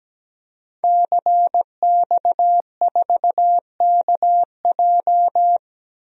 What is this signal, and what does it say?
Morse code, heard as a keyed tone, about 17 wpm, 700 Hz. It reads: CX4KJ
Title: DXCC-100 Call Signs - 5 Characters - 17wpm